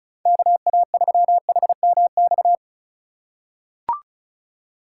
Morse code, heard as a keyed tone, about 34 wpm, 700 Hz. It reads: KA3HMX
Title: USA Call Signs - 6 Characters - 2x Character Spacing - 35wpm